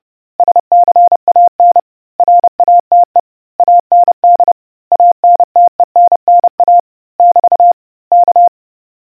Morse code, heard as a keyed tone, about 30 wpm, 700 Hz. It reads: SCAN RATE AND ANTENNA <BT> K